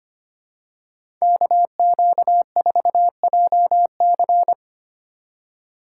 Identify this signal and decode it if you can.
Morse code, heard as a keyed tone, about 25 wpm, 700 Hz. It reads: KQ4JC
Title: USA Call Signs - 5 Characters - 2x Character Spacing - 25wpm